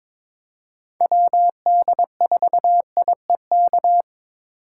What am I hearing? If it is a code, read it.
Morse code, heard as a keyed tone, about 23 wpm, 700 Hz. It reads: WD4IEK
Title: USA Call Signs - 6x Character Spacing - 22wpm